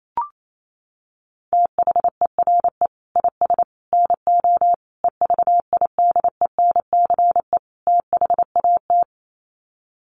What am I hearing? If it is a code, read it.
Morse code, heard as a keyed tone, about 28 wpm, 700 Hz. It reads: THERE IS NO EVIDENCE THAT
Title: Sets of 5 Words Encore 28wpm